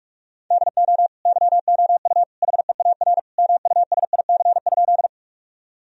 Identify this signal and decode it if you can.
Morse code, heard as a keyed tone, about 46 wpm, 700 Hz. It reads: DO YOU HEAR MUSIC?
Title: Sentences from Top 1000 Words - Review Part 1 - 45wpm